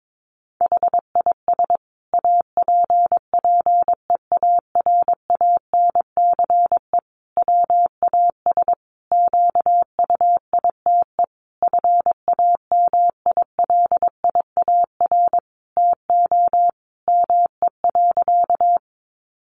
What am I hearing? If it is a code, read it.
Morse code, heard as a keyed tone, about 22 wpm, 700 Hz. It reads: HIS APPEARANCE WAS QUITE FAMILIAR TO ME.
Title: Sentences from Top 1100 Words 22wpm